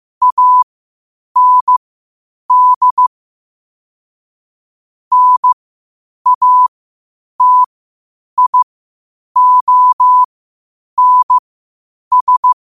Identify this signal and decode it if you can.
Morse code, heard as a keyed tone, about 15 wpm, spaced as for 8 wpm, 1000 Hz. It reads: AND NATIONS
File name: qotdfiles / 8wpm